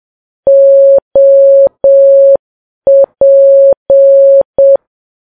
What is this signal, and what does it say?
Morse code, heard as a keyed tone, about 7 wpm, 550 Hz. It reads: OP
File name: Files Web tls / Morse on9fv / morse7kps